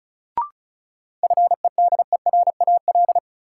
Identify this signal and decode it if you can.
Morse code, heard as a keyed tone, about 36 wpm, 700 Hz. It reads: FEDERAL
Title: Sets of 3 Words - New Word Drill - 35wpm